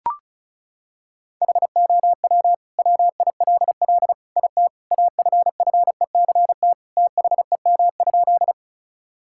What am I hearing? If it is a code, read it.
Morse code, heard as a keyed tone, about 36 wpm, 700 Hz. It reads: HOW WILL IT AFFECT THEM?
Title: Sentences from Top 1000 Words - Review Part 6 - 35wpm